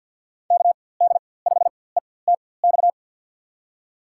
Morse code, heard as a keyed tone, about 51 wpm, spaced as for 23 wpm, 700 Hz. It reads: KD5ETX
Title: USA Call Signs - 6 Characters - 4x Character Spacing - 50wpm